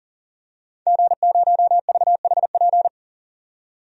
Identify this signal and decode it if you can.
Morse code, heard as a keyed tone, about 40 wpm, 700 Hz. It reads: G0VHP
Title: DXCC-100 Call Signs - 5 Characters - 40wpm